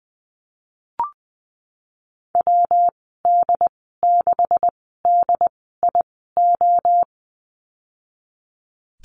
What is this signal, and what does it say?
Morse code, heard as a keyed tone, about 20 wpm, spaced as for 15 wpm, 700 Hz. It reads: WD6DIO
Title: 128 Lesson 21g - Number 6 - Call Signs 20wpm